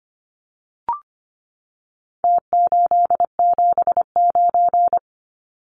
Morse code, heard as a keyed tone, about 25 wpm, 700 Hz. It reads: T879
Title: Numbers - Contest Style - 4-Digit 25wpm